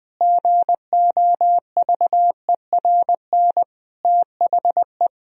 Morse code, heard as a keyed tone, about 21 wpm, 700 Hz. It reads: GOVERN THE
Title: Sentences from Top 1000 Words - Review Part 2 - 20wpm